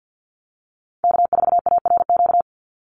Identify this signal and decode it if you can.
Morse code, heard as a keyed tone, about 48 wpm, 700 Hz. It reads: K4ARQ
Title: Call Signs Repeated 3x - 50wpm